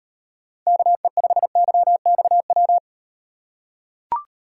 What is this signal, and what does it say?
Morse code, heard as a keyed tone, about 38 wpm, 700 Hz. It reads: KE5YXW